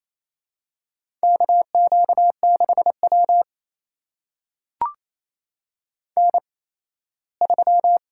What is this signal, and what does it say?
Morse code, heard as a keyed tone, about 28 wpm, spaced as for 17 wpm, 700 Hz. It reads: KQ6W N3
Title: USA Call Signs - 4 Characters - 8x Character Spacing - 28wpm